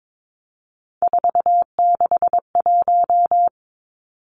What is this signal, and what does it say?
Morse code, heard as a keyed tone, about 22 wpm, 700 Hz. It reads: 461
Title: Three Challenging Letter-Number Combinations 22wpm